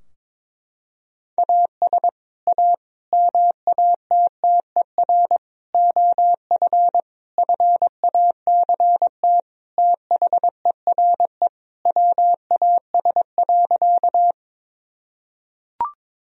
Morse code, heard as a keyed tone, about 22 wpm, 700 Hz. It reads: AS A MATTER OF FACT THERE WAS.
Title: Sentences from Top 1000 Words - Review Part 3 - 22wpm